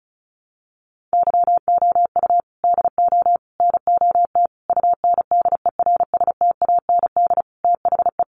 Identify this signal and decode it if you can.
Morse code, heard as a keyed tone, about 34 wpm, 700 Hz. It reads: YOU DO NOT UNDERSTAND THE
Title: Sentences from Top 600 Words 35wpm